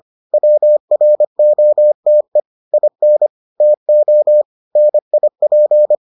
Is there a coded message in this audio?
Yes: Morse code from 0.3 to 6.0 s, heard as a keyed tone, about 25 wpm, 600 Hz. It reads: WROTE IN TO NIP